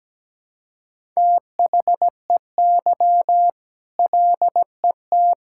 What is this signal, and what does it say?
Morse code, heard as a keyed tone, about 17 wpm, 700 Hz. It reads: THEY LET
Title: Sentences from Top 1000 Words - Review Part 5 - 17wpm